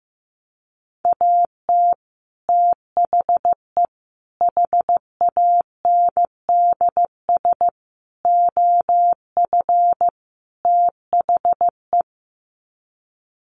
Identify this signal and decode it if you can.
Morse code, heard as a keyed tone, about 15 wpm, 700 Hz. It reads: AT THE HANDS OF THE
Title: Sets of 5 Words Encore2 15wpm